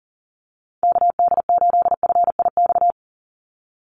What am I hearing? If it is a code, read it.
Morse code, heard as a keyed tone, about 39 wpm, 700 Hz. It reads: KD8FIX
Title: USA Call Signs 40wpm